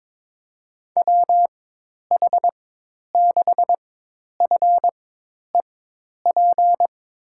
Morse code, heard as a keyed tone, about 21 wpm, spaced as for 10 wpm, 700 Hz. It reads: WH6FEP
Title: USA Call Signs - 6 Characters - 4x Character Spacing - 22wpm